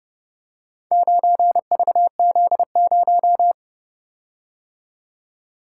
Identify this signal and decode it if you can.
Morse code, heard as a keyed tone, about 30 wpm, 700 Hz. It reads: 9VZ0